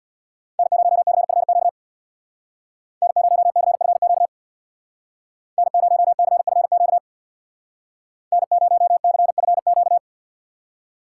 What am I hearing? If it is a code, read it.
Morse code, heard as a keyed tone, about 65 wpm, 700 Hz. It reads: N0CFQ N0CFQ N0CFQ N0CFQ